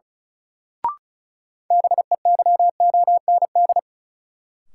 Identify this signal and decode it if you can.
Morse code, heard as a keyed tone, about 34 wpm, 700 Hz. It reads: BEYOND
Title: Sentences from Top 1100 Words - New Word Drill Alt - 35wpm